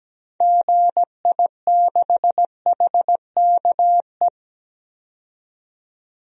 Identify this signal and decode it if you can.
Morse code, heard as a keyed tone, about 17 wpm, 700 Hz. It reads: GI6HKE
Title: DXCC-100 Call Signs - 6 Characters - 17wpm